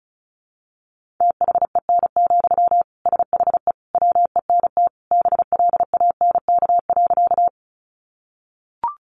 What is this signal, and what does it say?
Morse code, heard as a keyed tone, about 34 wpm, 700 Hz. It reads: THEN, SHE WENT BLANK.
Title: Sentences from Top 1500 Words 35wpm